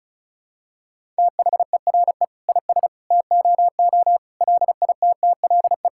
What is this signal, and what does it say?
Morse code, heard as a keyed tone, about 36 wpm, 700 Hz. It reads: THERE IS TOO LITTLE
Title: Sentences from Top 1000 Words - Review Part 5 - 35wpm